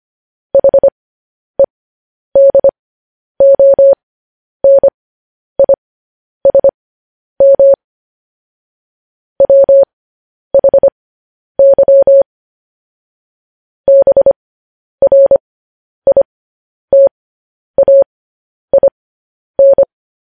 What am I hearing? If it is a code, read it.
Morse code, heard as a keyed tone, about 25 wpm, spaced as for 10 wpm, 550 Hz. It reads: HEDONISM WHY BRITAIN
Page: news-headlines-in-morse-code-at-10-wpm